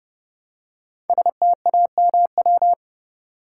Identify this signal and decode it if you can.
Morse code, heard as a keyed tone, about 30 wpm, 700 Hz. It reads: STAMW
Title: Code Groups - 5 Letters - 30wpm